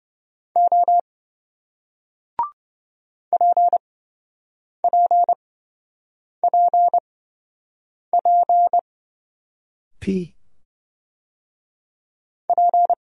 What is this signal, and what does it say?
Morse code, heard as a keyed tone, about 28 wpm, spaced as for 7 wpm, 700 Hz. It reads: O PPPP P